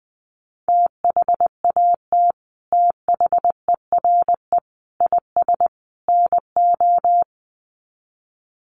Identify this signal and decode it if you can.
Morse code, heard as a keyed tone, about 20 wpm, 700 Hz. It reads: THAT THERE IS NO
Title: Sets of 4 Words 20wpm